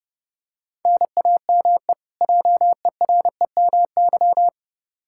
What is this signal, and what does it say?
Morse code, heard as a keyed tone, about 30 wpm, 700 Hz. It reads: NAME JEREMY
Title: K1USN SST QSO Practice 30wpm